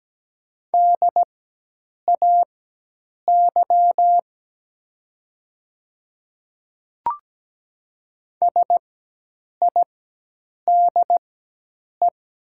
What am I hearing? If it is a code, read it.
Morse code, heard as a keyed tone, about 17 wpm, spaced as for 8 wpm, 700 Hz. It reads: DAY SIDE
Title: 4x Character Spacing - Top 100 Words - 17wpm